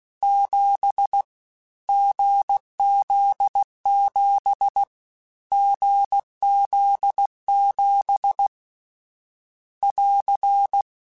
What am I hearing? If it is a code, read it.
Morse code, heard as a keyed tone, about 16 wpm, 800 Hz. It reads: 7 GZ7 GZ7 <AR>
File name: Short Runs / Short Runs-mp3s / awj1ndb6gz7 @15